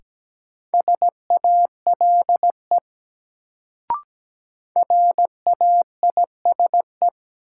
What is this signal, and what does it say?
Morse code, heard as a keyed tone, about 17 wpm, 700 Hz. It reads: SALE RAISE
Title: Sentences from Top 1400 Words - New Word Drill - 17wpm